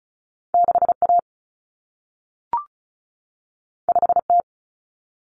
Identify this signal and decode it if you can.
Morse code, heard as a keyed tone, about 36 wpm, 700 Hz. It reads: 6A 5T
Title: Common Cut Numbers - 2-Digit 35wpm